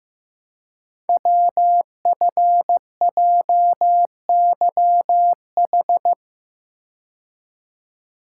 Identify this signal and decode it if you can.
Morse code, heard as a keyed tone, about 15 wpm, 700 Hz. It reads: WFJYH
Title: Code Groups - 5 Letters - 15wpm